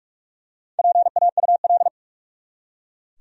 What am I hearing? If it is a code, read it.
Morse code, heard as a keyed tone, about 44 wpm, 700 Hz. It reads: PAUL
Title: USA Common Names - Max 4 Letters - 45wpm